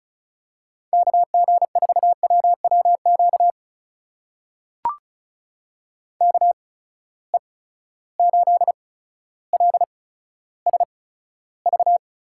Morse code, heard as a keyed tone, about 36 wpm, spaced as for 15 wpm, 700 Hz. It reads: KG4WWQ KE8LSV
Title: USA Call Signs - 6 Characters - 8x Character Spacing - 35wpm